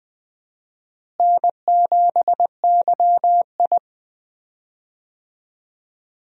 Morse code, heard as a keyed tone, about 20 wpm, 700 Hz. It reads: N7YI
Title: USA Call Signs - 4 Characters - Repeated 2x - 20wpm